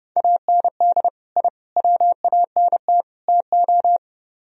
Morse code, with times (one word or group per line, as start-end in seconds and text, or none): 0.2-1.1 AND
1.4-1.5 I
1.8-3.0 WANT
3.3-4.0 TO